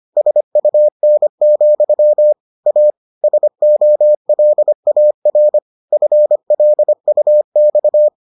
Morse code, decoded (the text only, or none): SUN, A SOLAR FLUX